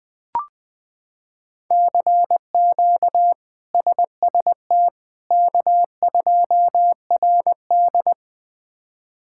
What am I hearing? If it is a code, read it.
Morse code, heard as a keyed tone, about 20 wpm, 700 Hz. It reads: CQ SST K2RD
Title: K1USN SST QSO Practice - 20wpm